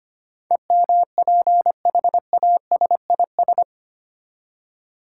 Morse code, heard as a keyed tone, about 25 wpm, 700 Hz. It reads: EMPHASIS
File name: Eight Letter Words - 25wpm